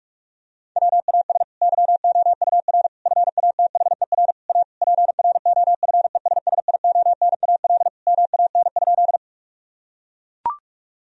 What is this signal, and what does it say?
Morse code, heard as a keyed tone, about 42 wpm, 700 Hz. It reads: WAS YOUR FATHER A PROFESSIONAL MAN?
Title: Sentences from Top 1000 Words - Review Part 4 - 45wpm